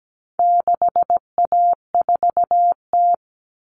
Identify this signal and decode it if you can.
Morse code, heard as a keyed tone, about 17 wpm, 700 Hz. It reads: 6A4T
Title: All Cut Numbers - 4-Digit 17wpm